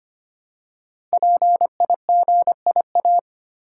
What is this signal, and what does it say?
Morse code, heard as a keyed tone, about 25 wpm, 700 Hz. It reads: PIGIA
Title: Code Groups - 5 Letters - 25wpm